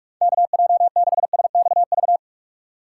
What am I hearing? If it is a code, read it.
Morse code, heard as a keyed tone, about 59 wpm, 700 Hz. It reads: KJ6SXV